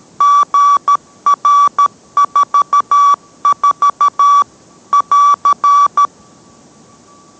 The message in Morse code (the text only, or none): GR44 <AR>